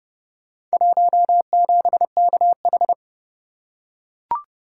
Morse code, heard as a keyed tone, about 28 wpm, 700 Hz. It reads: 17KH